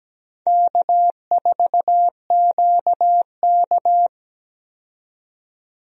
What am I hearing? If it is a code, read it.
Morse code, heard as a keyed tone, about 17 wpm, 700 Hz. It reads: K4QK
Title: USA Call Signs - 4 Characters - 17wpm